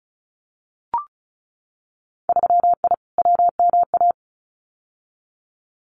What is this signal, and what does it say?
Morse code, heard as a keyed tone, about 36 wpm, 700 Hz. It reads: 3I WMA
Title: Winter Field Day Exchanges Encore 35wpm